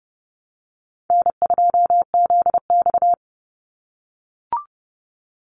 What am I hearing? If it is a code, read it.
Morse code, heard as a keyed tone, about 30 wpm, 700 Hz. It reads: N2ZX
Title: Call Signs - 30wpm